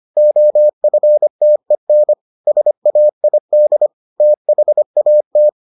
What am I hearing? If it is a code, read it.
Morse code, heard as a keyed tone, about 25 wpm, 600 Hz. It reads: OFTEN SAID THAT